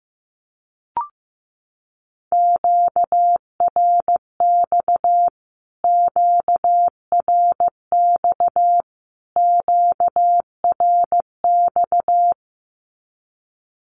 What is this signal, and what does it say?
Morse code, heard as a keyed tone, about 15 wpm, 700 Hz. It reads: QRX QRX QRX
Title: Q-Codes Repeated 3x 15wpm